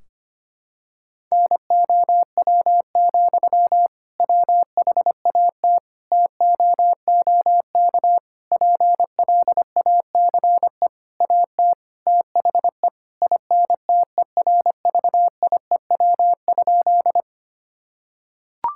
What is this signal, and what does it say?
Morse code, heard as a keyed tone, about 25 wpm, 700 Hz. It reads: NOW, WHAT TOOK PLACE AT THE INTERVIEW?
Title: Sentences from Top 1600 Words 25wpm